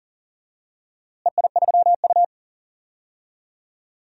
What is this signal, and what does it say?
Morse code, heard as a keyed tone, about 40 wpm, 700 Hz. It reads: EI3U
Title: DXCC-100 Call Signs - 4 Characters - Repeated 3x - 40wpm